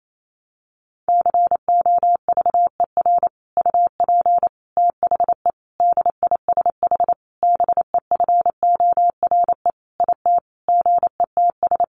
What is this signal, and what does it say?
Morse code, heard as a keyed tone, about 28 wpm, 700 Hz. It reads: COVER UP THE DISH BEFORE IT GETS
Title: Sentences from Top 1300 Words 28wpm